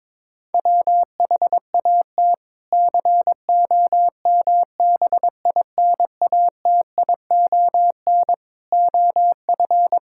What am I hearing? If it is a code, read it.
Morse code, heard as a keyed tone, about 22 wpm, 700 Hz. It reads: WHAT COMBINATION OF
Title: Sentences from Top 1000 Words - Review Part 6 - 22wpm